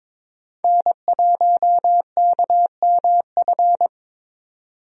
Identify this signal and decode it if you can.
Morse code, heard as a keyed tone, about 22 wpm, 700 Hz. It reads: N1KMF
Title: USA Call Signs - 5 Characters - 22wpm